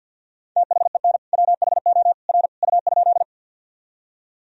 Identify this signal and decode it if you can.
Morse code, heard as a keyed tone, about 48 wpm, 700 Hz. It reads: THEN WHO R U?
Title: Sentences from Top 100 Words and Abbreviations 50wpm